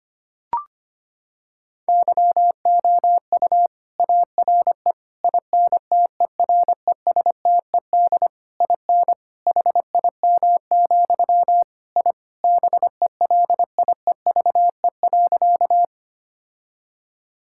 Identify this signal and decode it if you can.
Morse code, heard as a keyed tone, about 25 wpm, 700 Hz. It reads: YOU ARE INTERESTED IN HIM, I BELIEVE.
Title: Sentences from Top 1000 Words - Review Part 6 - 25wpm